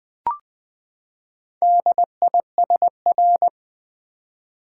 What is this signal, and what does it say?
Morse code, heard as a keyed tone, about 20 wpm, 700 Hz. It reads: DISR
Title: Build A Word - 4 Letters At A Time - 10-Letter Words - 20wpm